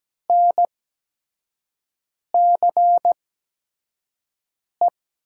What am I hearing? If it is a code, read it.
Morse code, heard as a keyed tone, about 17 wpm, spaced as for 5 wpm, 700 Hz. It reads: NCE
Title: USA Call Signs - 6 Characters - 8x Character Spacing - 17wpm